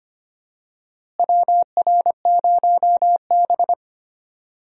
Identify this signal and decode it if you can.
Morse code, heard as a keyed tone, about 25 wpm, 700 Hz. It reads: WR0B
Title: USA Call Signs - 4 Characters - Repeated 2x - 25wpm